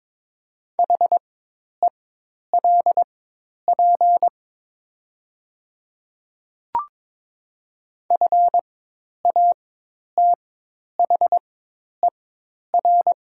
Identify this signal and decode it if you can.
Morse code, heard as a keyed tone, about 22 wpm, spaced as for 10 wpm, 700 Hz. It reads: HELP FATHER
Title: 4x Character Spacing - Top 200 Words - 22wpm